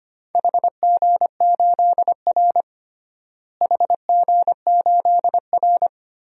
Morse code, heard as a keyed tone, about 25 wpm, 700 Hz. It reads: HG8R HG8R
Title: DXCC-100 Call Signs - 4 Characters - Repeated 2x - 25wpm